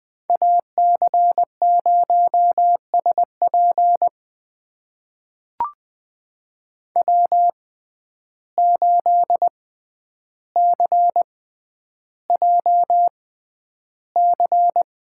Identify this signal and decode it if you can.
Morse code, heard as a keyed tone, about 20 wpm, spaced as for 10 wpm, 700 Hz. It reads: AC0SP W8CJC
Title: USA Call Signs - 5 Characters - 6x Character Spacing - 20wpm